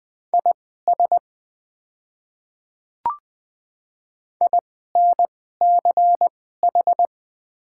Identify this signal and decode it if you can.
Morse code, heard as a keyed tone, about 20 wpm, spaced as for 14 wpm, 700 Hz. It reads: IS INCH
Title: 2x Character Spacing - Top 500 Words - 20wpm